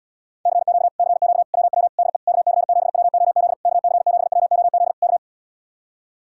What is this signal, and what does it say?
Morse code, heard as a keyed tone, about 77 wpm, 700 Hz. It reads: CQ CQ WW DE KC7WOZ KC7WOZ K